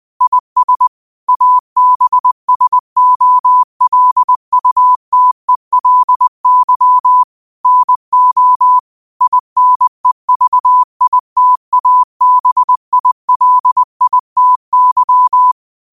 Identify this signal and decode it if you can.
Morse code, heard as a keyed tone, about 20 wpm, 1000 Hz. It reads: IS ABSOLUTELY NO INEVITABILITY
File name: qotdfiles / 20wpm